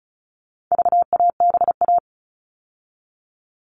Morse code, heard as a keyed tone, about 34 wpm, 700 Hz. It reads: VABA
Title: All Cut Numbers - 4-Digit 35wpm